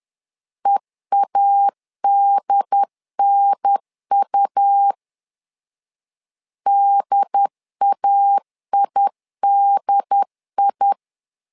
Morse code, heard as a keyed tone, about 10 wpm, 800 Hz. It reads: EADNU DAIDI
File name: LEZIONI / 3 MWRO / 3 MW 10wpm 1959